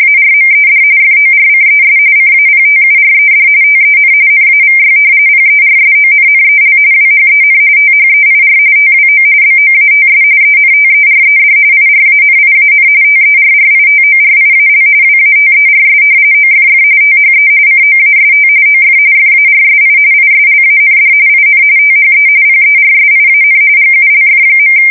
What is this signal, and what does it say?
RTTY, about 45.45 baud. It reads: ARMED FORCES CAPTURED TEN NAVY PERSONNEL WHEN THEIR TWO BOATS ENTERED IRANIAN TERRITORIAL WATERS OFF THE COAST OF FARSI ISLAND IN THE PERSIAN GULF.
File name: fsk170 / usnavy5